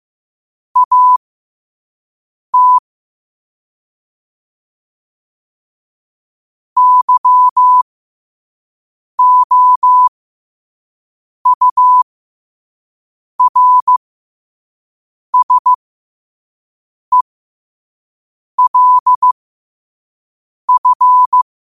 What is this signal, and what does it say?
Morse code, heard as a keyed tone, about 15 wpm, spaced as for 5 wpm, 1000 Hz. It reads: AT YOURSELF
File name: qotdfiles / 5wpm